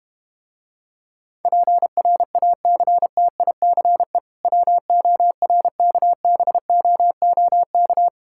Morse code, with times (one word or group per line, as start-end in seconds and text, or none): 1.4-4.2 PRACTICE
4.5-8.1 WORKBOOK